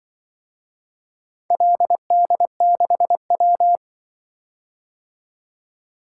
Morse code, heard as a keyed tone, about 25 wpm, 700 Hz. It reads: LD6W
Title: Speed-Racing - Four Letter-Number Combinations - 20wpm